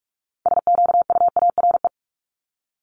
Morse code, heard as a keyed tone, about 46 wpm, 700 Hz. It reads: SQUARE
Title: Multi-letter Phonemes 45wpm